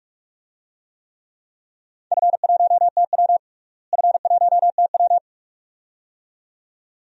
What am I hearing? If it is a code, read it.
Morse code, heard as a keyed tone, about 46 wpm, 700 Hz. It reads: F1TW F1TW
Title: DXCC-100 Call Signs - 4 Characters - Repeated 2x - 45wpm